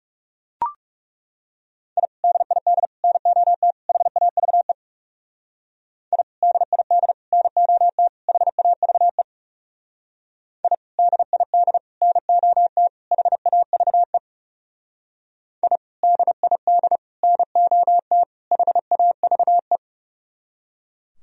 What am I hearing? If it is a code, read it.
Morse code, heard as a keyed tone, about 37 wpm, 700 Hz. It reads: I DID NOT HAVE I DID NOT HAVE I DID NOT HAVE I DID NOT HAVE
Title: Speed-Racing - Sets of 4 Words 30wpm